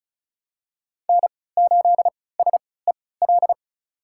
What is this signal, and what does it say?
Morse code, heard as a keyed tone, about 34 wpm, spaced as for 20 wpm, 700 Hz. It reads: N8SEL
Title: USA Call Signs - 5 Characters - 3x Character Spacing - 35wpm